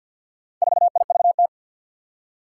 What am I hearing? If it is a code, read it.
Morse code, heard as a keyed tone, about 50 wpm, 700 Hz. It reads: 4IVT
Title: Four Challenging Letter-Number Combinations 50wpm